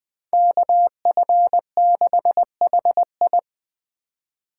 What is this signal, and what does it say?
Morse code, heard as a keyed tone, about 20 wpm, 700 Hz. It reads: KF6HI
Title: 2023 SOTA Chaser Call Signs - 20wpm